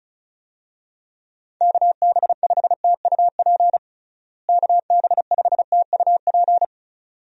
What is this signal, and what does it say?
Morse code, heard as a keyed tone, about 36 wpm, 700 Hz. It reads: KB5TUP KB5TUP
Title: USA Call Signs - 6 Characters - Repeated 2x - 35wpm